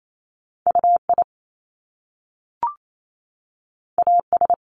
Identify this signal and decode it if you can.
Morse code, heard as a keyed tone, about 28 wpm, 700 Hz. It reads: UI AS
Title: Top Two Letter Combinations in Words 28wpm